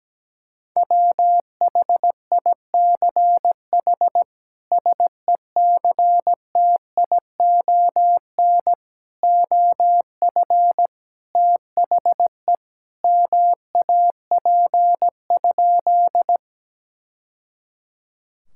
Morse code, heard as a keyed tone, about 17 wpm, 700 Hz. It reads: WHICH SECTION OF THE MAP?
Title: Sentences from Top 1000 Words - Review Part 5 - 17wpm